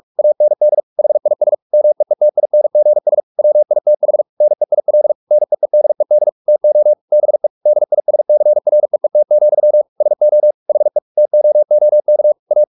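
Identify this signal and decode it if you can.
Morse code, heard as a keyed tone, about 44 wpm, 600 Hz. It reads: AND HIS MEETINGS WITH NEIL NEEDED TO BE DISCREET, SO HE TOOK A
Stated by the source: Morse practice at 45 wpm, the novel Never by Ken Follett in English, from CW med Gnister.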